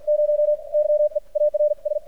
Morse code, heard as a keyed tone, about 55 wpm, spaced as for 43 wpm, 600 Hz. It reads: 0 YEARS
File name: archived / CRSnet